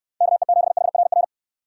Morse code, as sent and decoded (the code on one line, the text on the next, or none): -.- . --... .... .-. .-.
KE7HRR